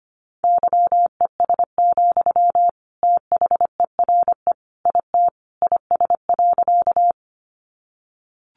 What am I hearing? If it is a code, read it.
Morse code, heard as a keyed tone, about 25 wpm, 700 Hz. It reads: YES, THERE IT IS.
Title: Sentences from Top 500 Words 25wpm